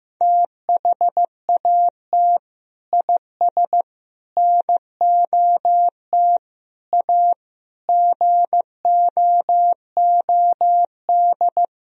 Morse code, heard as a keyed tone, about 16 wpm, 700 Hz. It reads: THAT IS NOT A GOOD
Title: Sentences from Top 1000 Words - Review Part 3 - 15wpm